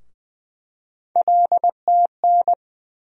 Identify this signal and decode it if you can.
Morse code, heard as a keyed tone, about 20 wpm, 700 Hz. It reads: LTN
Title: USA Call Sign Postfixes - 3 Letters 20wpm